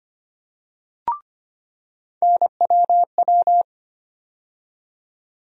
Morse code, heard as a keyed tone, about 25 wpm, 700 Hz. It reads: NWW
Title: USA Call Sign Postfixes - 3 Letters 25wpm